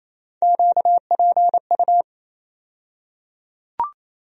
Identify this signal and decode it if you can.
Morse code, heard as a keyed tone, about 28 wpm, 700 Hz. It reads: QPU
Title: USA Call Sign Postfixes - 3 Letters 28wpm